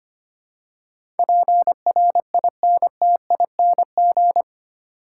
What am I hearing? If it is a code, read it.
Morse code, heard as a keyed tone, about 25 wpm, 700 Hz. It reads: PRINTING
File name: 8 Letter Words - First 3 Letters Given - 25wpm